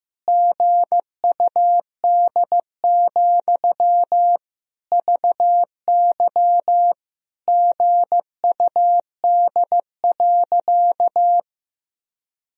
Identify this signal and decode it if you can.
Morse code, heard as a keyed tone, about 15 wpm, 700 Hz. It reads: GUD, VY GUD.
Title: Sentences from Top 200 Words and Abbreviations 15wpm